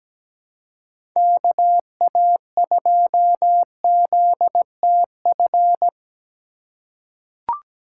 Morse code, heard as a keyed tone, about 17 wpm, 700 Hz. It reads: KA2ZTF
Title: USA Call Signs - 6 Characters - 4x Character Spacing - 17wpm